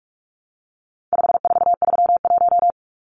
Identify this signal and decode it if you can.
Morse code, heard as a keyed tone, about 44 wpm, 700 Hz. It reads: 5431
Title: Numbers - 4-Digit 45wpm